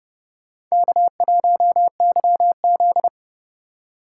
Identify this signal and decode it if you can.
Morse code, heard as a keyed tone, about 28 wpm, 700 Hz. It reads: K1YZ